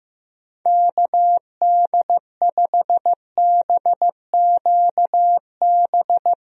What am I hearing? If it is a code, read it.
Morse code, heard as a keyed tone, about 16 wpm, 700 Hz. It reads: KD5BQB
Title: USA Call Signs - 6 Characters - Repeated 2x - 15wpm